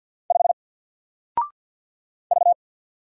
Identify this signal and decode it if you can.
Morse code, heard as a keyed tone, about 50 wpm, 700 Hz. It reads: 5 V
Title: Single Letter-Number Alt 50wpm